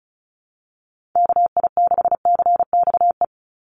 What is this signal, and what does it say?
Morse code, heard as a keyed tone, about 34 wpm, 700 Hz. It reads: KI6CXE
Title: USA Call Signs 35wpm